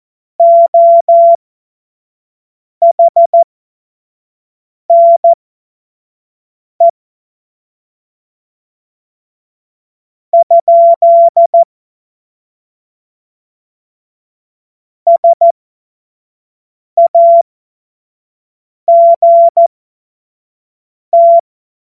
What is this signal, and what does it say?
Morse code, heard as a keyed tone, about 14 wpm, spaced as for 4 wpm, 680 Hz. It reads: OHNE? SAGT
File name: Morsekurs / Witze / Witz 05. Oktober 2021, 25 F70